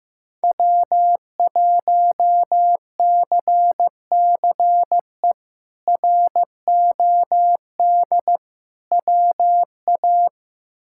Morse code, heard as a keyed tone, about 15 wpm, 700 Hz. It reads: W1CCE ROD WA
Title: K1USN SST QSO Practice - 15wpm